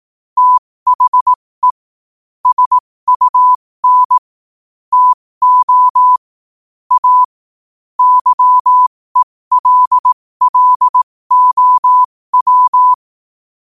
Morse code, heard as a keyed tone, about 18 wpm, spaced as for 15 wpm, 1000 Hz. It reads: THE SUN TO A YELLOW
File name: qotdfiles / 15wpm